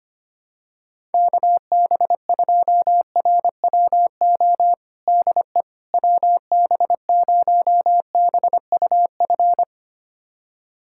Morse code, heard as a keyed tone, about 25 wpm, 700 Hz. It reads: KB2RWO DE WB0BUF
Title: USA Callsign DE Callsign - 6 Characters - 25wpm